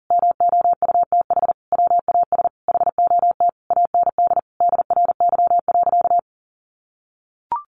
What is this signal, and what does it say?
Morse code, heard as a keyed tone, about 37 wpm, 700 Hz. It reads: MOUTH WAS HOT AND DRY.
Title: Sentences from Top 1500 Words 40wpm